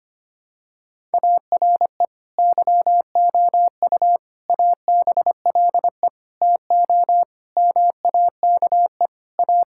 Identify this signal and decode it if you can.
Morse code, heard as a keyed tone, about 25 wpm, 700 Hz. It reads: ARE YOU ABLE TO MAKE A
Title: Sentences from Top 1000 Words - Review Part 6 - 25wpm